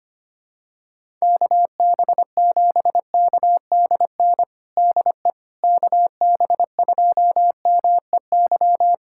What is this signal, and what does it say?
Morse code, heard as a keyed tone, about 25 wpm, 700 Hz. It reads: KB7KDN DE KB2MEY
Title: USA Callsign DE Callsign - 6 Characters - 25wpm